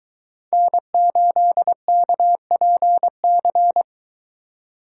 Morse code, heard as a keyed tone, about 23 wpm, 700 Hz. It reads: N8KPC